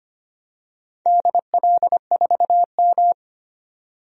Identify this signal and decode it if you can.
Morse code, heard as a keyed tone, about 25 wpm, 700 Hz. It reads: DL4M
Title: DXCC-100 Call Signs - 4 Characters - 25wpm